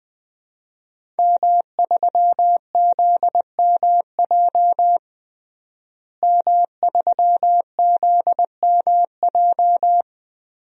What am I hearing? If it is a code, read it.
Morse code, heard as a keyed tone, about 20 wpm, 700 Hz. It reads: M3ZMJ M3ZMJ
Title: DXCC-100 Call Signs - 5 Characters - Repeated 3x - 20wpm